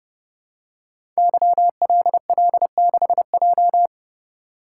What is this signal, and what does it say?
Morse code, heard as a keyed tone, about 31 wpm, 700 Hz. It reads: YLL6J
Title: Code Groups - 5 Letter:Numbers - 30wpm